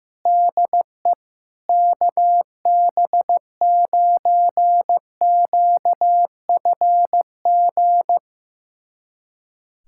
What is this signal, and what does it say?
Morse code, heard as a keyed tone, about 14 wpm, 700 Hz. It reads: DE KB9QFG
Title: USA Callsign DE Callsign - 6 Characters - 15wpm